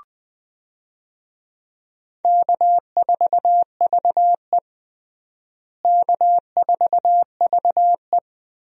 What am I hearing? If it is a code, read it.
Morse code, heard as a keyed tone, about 20 wpm, 700 Hz. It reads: K4VE K4VE
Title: USA Call Signs - 4 Characters - Repeated 2x - 20wpm